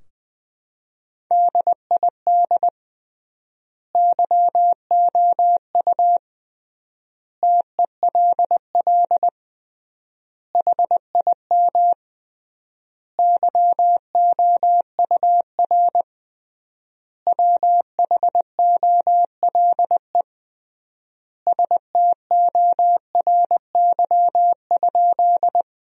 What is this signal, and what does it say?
Morse code, heard as a keyed tone, about 20 wpm, 700 Hz. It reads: DID YOU TELL HIM YOUR WHOLE STORY?
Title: 3x Word-Spacing - Sentences from Top 400 Words - 20wpm